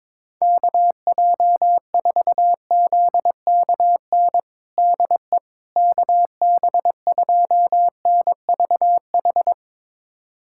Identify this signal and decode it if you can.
Morse code, heard as a keyed tone, about 22 wpm, 700 Hz. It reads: KJ4ZKN DE KB2NVH
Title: USA Callsign DE Callsign - 6 Characters - 22wpm